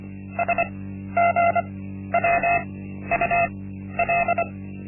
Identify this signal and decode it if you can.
Morse code, heard as a keyed tone, about 25 wpm, spaced as for 12 wpm, 690 Hz. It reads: SGWUL
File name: Linksmakalnis-1